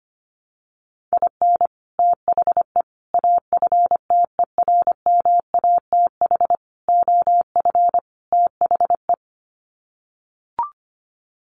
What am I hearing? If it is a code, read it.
Morse code, heard as a keyed tone, about 25 wpm, 700 Hz. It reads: IN THE AFTERMATH OF THE
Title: Sets of 5 Words 25wpm